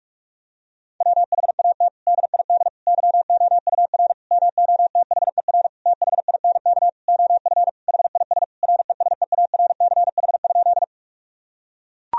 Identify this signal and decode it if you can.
Morse code, heard as a keyed tone, about 41 wpm, 700 Hz. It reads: WHAT DID YOUR MOTHER THINK OF HIS RESEARCH?
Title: Sentences from Top 1000 Words - Review Part 4 - 45wpm